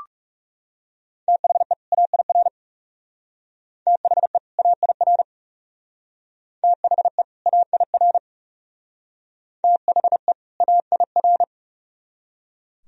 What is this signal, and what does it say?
Morse code, heard as a keyed tone, about 37 wpm, 700 Hz. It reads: THE AIR THE AIR THE AIR THE AIR
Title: Speed-Racing - Sets of 2 Words 30wpm